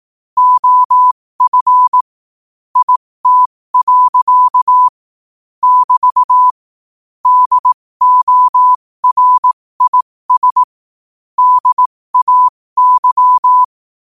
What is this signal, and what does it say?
Morse code, heard as a keyed tone, about 18 wpm, spaced as for 15 wpm, 1000 Hz. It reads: OF IT. <BT> DORIS DAY
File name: qotdfiles / 15wpm